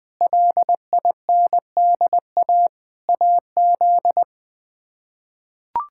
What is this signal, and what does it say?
Morse code, heard as a keyed tone, about 20 wpm, 700 Hz. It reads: LINDA AZ
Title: K1USN SST QSO Practice - 20wpm